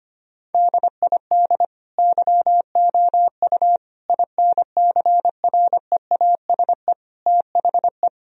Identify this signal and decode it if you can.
Morse code, heard as a keyed tone, about 25 wpm, 700 Hz. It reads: DID YOU INCREASE THE
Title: Sentences from Top 1000 Words - Review Part 4 - 25wpm